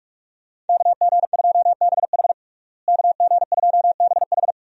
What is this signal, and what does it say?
Morse code, heard as a keyed tone, about 48 wpm, 700 Hz. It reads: KG2BH KG2BH
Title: USA Call Signs - 5 Characters - Repeated 3x - 45wpm